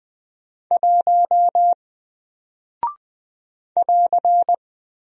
Morse code, heard as a keyed tone, about 20 wpm, 700 Hz. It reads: 1 <AR>
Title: Single Character (Letter:Number:Punctuation:Prosign) Alt 20wpm